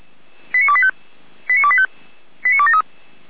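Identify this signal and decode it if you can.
Five-tone selective call (ZVEI), heard as quick runs of five tones, tones of about 70 ms. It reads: 89285 89285 89273
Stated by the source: ZVEI five-tone call received on SDR, 455 channel, logged by feuerwehr-orlamuende.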